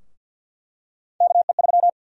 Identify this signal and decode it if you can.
Morse code, heard as a keyed tone, about 50 wpm, 700 Hz. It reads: KE3
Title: USA Call Sign Prefixes - 50wpm